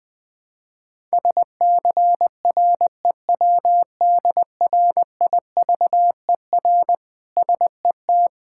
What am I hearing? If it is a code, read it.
Morse code, heard as a keyed tone, about 20 wpm, 700 Hz. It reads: SCREWDRIVER SET
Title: Difficult Words with Hints - 20wpm